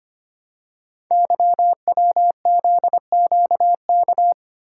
Morse code, heard as a keyed tone, about 25 wpm, 700 Hz. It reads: YWZQK
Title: Code Groups - 5 Letters - 25wpm